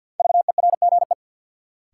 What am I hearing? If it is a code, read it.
Morse code, heard as a keyed tone, about 48 wpm, 700 Hz. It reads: VERGE
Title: Sets of 4 Words - New Word Drill - 50wpm